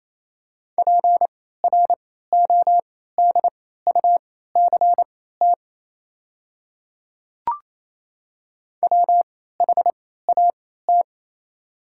Morse code, heard as a keyed tone, about 28 wpm, spaced as for 16 wpm, 690 Hz. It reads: PRODUCT WHAT
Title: 3x Character Spacing - Top 500 Words - 28wpm